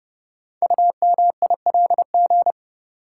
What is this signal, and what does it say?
Morse code, heard as a keyed tone, about 30 wpm, 700 Hz. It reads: UMILG
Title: Code Groups - 5 Letters - 30wpm